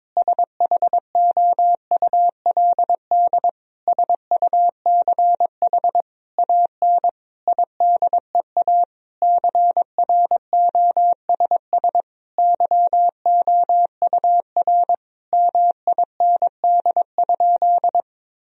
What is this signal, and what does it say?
Morse code, heard as a keyed tone, about 22 wpm, 700 Hz. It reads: SHOULD SUCH AN IDEA CROSS YOUR MIND?
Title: Sentences from Top 1000 Words - Review Part 1 - 22wpm